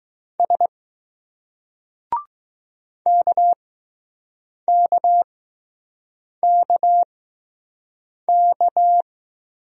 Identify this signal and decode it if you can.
Morse code, heard as a keyed tone, about 20 wpm, spaced as for 6 wpm, 700 Hz. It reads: S KKKK